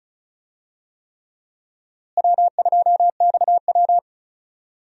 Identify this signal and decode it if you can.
Morse code, heard as a keyed tone, about 36 wpm, 700 Hz. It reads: W2XW
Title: USA Call Signs - 4 Characters - Repeated 3x - 35wpm